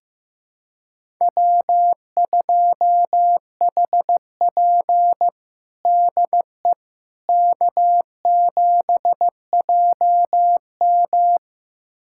Morse code, heard as a keyed tone, about 15 wpm, 700 Hz. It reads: W2HP DE K7JM
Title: USA Callsign DE Callsign - 4 Characters - 15wpm